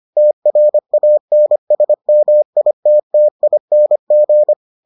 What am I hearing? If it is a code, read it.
Morse code, heard as a keyed tone, about 25 wpm, 600 Hz. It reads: TRANSMITTING